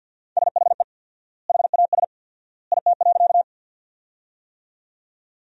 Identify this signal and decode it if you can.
Morse code, heard as a keyed tone, about 50 wpm, 700 Hz. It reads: SHE HAS IT.
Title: 4x Word-Spacing - Sentences from Top 100 Words - 50wpm